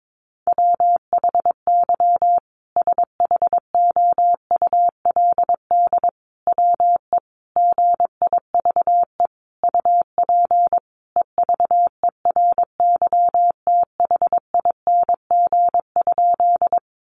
Morse code, heard as a keyed tone, about 22 wpm, 700 Hz. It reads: WHY SHOULD WE GIVE UP EVERYTHING?
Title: Sentences from Top 700 Words 22wpm